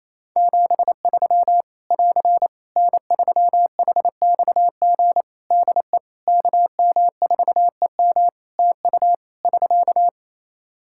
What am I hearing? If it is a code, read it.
Morse code, heard as a keyed tone, about 28 wpm, 700 Hz. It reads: 73 <AR> N3HXG DE KM4EM TU <SK>
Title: Practice QSO - Exchange 4 - Ending - 28wpm